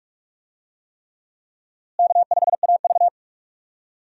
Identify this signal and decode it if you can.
Morse code, heard as a keyed tone, about 46 wpm, 700 Hz. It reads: K5AV
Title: USA Call Signs - 4 Characters - Repeated 3x - 45wpm